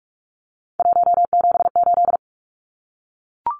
Morse code, heard as a keyed tone, about 44 wpm, 700 Hz. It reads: 178
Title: Numbers - 3-Digit 45wpm